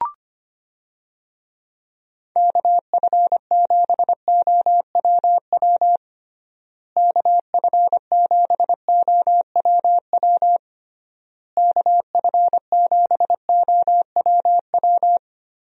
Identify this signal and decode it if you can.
Morse code, heard as a keyed tone, about 25 wpm, 700 Hz. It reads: KF7OWW KF7OWW KF7OWW
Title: USA Call Signs - 6 Characters - Repeated 3x - 25wpm